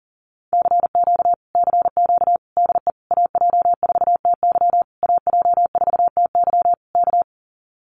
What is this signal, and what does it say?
Morse code, heard as a keyed tone, about 40 wpm, 700 Hz. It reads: CQ CQ DE AJ4TY AJ4TY K
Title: Common QSO Phrases Encore 40wpm